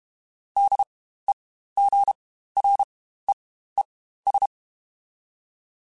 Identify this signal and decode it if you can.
Morse code, heard as a keyed tone, about 32 wpm, spaced as for 15 wpm, 800 Hz. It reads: DEGREES
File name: Spaced Characters / Spaced Characters-mp3s / Weather4 10 sp @30